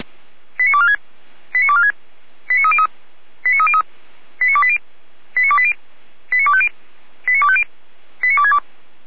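Five-tone selective call (ZVEI), heard as quick runs of five tones, tones of about 70 ms. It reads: 89246 89246 89303 89303 89280 89280 89250 89250 89262
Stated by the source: ZVEI five-tone call 89220 received by SDR.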